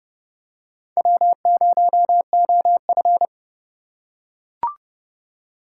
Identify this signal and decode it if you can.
Morse code, heard as a keyed tone, about 31 wpm, 700 Hz. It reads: W0OF